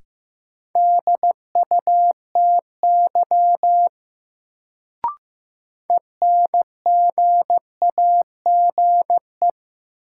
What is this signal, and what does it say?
Morse code, heard as a keyed tone, about 15 wpm, 700 Hz. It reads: DUTY ENGAGE
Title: Sentences from Top 1500 Words - New Word Drill - 15wpm